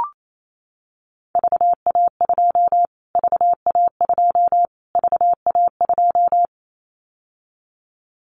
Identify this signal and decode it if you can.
Morse code, heard as a keyed tone, about 28 wpm, 700 Hz. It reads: VA2 VA2 VA2
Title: Callsign Prefixes for Top 25 DXCC - Repeated 3x - 28wpm